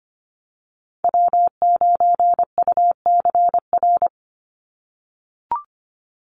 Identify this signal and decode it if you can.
Morse code, heard as a keyed tone, about 25 wpm, 700 Hz. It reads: W9UCR
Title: Call Signs - 25wpm